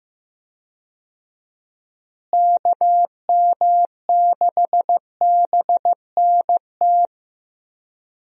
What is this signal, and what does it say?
Morse code, heard as a keyed tone, about 15 wpm, 700 Hz. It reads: KM6BNT
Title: USA Call Signs - 6 Characters - Repeated 2x - 15wpm